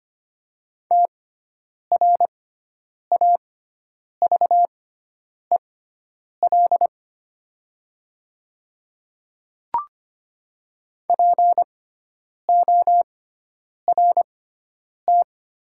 Morse code, heard as a keyed tone, about 25 wpm, spaced as for 9 wpm, 700 Hz. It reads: TRAVEL PORT
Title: 6x Character Spacing - Top 500 Words - 25wpm